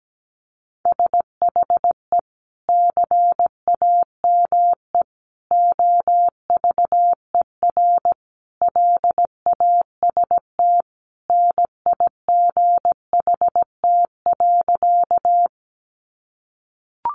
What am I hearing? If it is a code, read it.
Morse code, heard as a keyed tone, about 17 wpm, 700 Hz. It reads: SHE CAME OVER LAST NIGHT.
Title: Sentences from Top 300 Words 17wpm